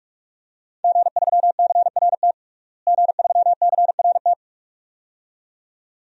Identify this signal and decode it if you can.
Morse code, heard as a keyed tone, about 41 wpm, 700 Hz. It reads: G3CRT G3CRT
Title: DXCC-100 Call Signs - 5 Characters - Repeated 2x - 45wpm